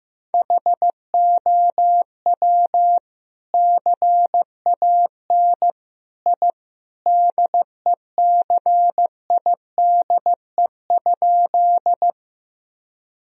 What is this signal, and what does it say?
Morse code, heard as a keyed tone, about 15 wpm, 700 Hz. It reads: HOW CAN I DECIDE?
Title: Sentences from Top 1000 Words - Review Part 3 - 15wpm